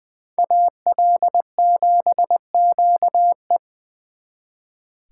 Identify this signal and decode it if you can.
Morse code, heard as a keyed tone, about 24 wpm, 700 Hz. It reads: AL7QE
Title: Speed Racing - USA 2x2 Call Signs 20wpm